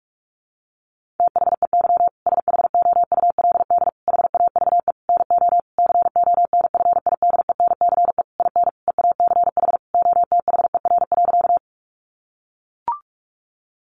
Morse code, heard as a keyed tone, about 44 wpm, 700 Hz. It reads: THEY SHOULD HAVE NO CONFIDENCE IN EACH OTHER.
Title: Sentences from Top 1500 Words 45wpm